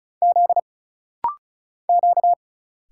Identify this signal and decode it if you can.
Morse code, heard as a keyed tone, about 36 wpm, 700 Hz. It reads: Z Q
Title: Single Letter-Number - Rapid-Fire 35wpm